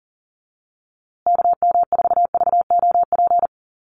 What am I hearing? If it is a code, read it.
Morse code, heard as a keyed tone, about 40 wpm, 700 Hz. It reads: KM4VOP
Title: USA Call Signs Repeated 2x 40wpm